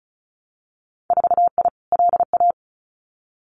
Morse code, heard as a keyed tone, about 36 wpm, 700 Hz. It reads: 4I LA
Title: Winter Field Day Exchanges Encore 35wpm